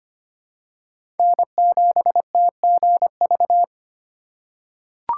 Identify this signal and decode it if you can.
Morse code, heard as a keyed tone, about 25 wpm, 700 Hz. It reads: N7TGV
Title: USA Call Signs - 5 Characters - 3x Character Spacing - 25wpm